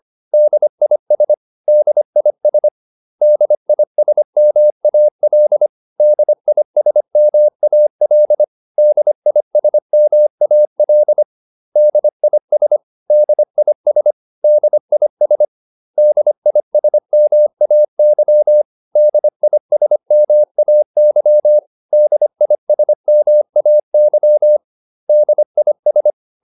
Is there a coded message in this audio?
Yes: Morse code from 0.3 to 26.1 s, heard as a keyed tone, about 25 wpm, 600 Hz. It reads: DIS DIS DISMAL DISMAL DISMAL DIS DIS DIS DISMAY DISMAY DISMAY DIS